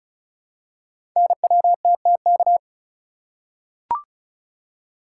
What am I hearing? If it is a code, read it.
Morse code, heard as a keyed tone, about 36 wpm, 690 Hz. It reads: NWTTK
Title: Code Groups - 5 Letters - 35wpm